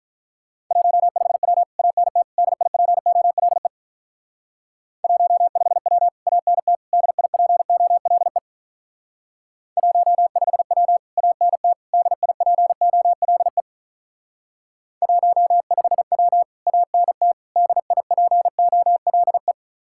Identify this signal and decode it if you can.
Morse code, heard as a keyed tone, about 44 wpm, 700 Hz. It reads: 15W ANT DIPOLE 15W ANT DIPOLE 15W ANT DIPOLE 15W ANT DIPOLE